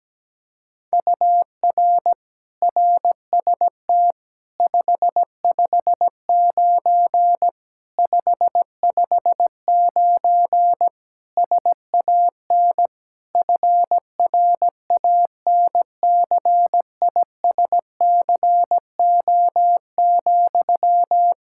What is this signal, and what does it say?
Morse code, heard as a keyed tone, about 17 wpm, 700 Hz. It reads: UR RST 559 559 SAN FRANCISCO,
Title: Practice QSO - Exchange 1 - Beginning - 17wpm